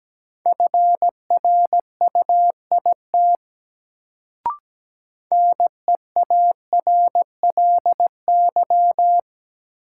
Morse code, heard as a keyed tone, about 17 wpm, 700 Hz. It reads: FRUIT NEARLY
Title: Sentences from Top 1300 Words - New Word Drill Alt - 17wpm